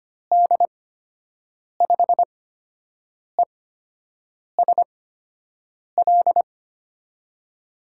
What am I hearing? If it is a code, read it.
Morse code, heard as a keyed tone, about 25 wpm, spaced as for 7 wpm, 700 Hz. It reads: D5ESL
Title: USA Call Signs - 6 Characters - 8x Character Spacing - 25wpm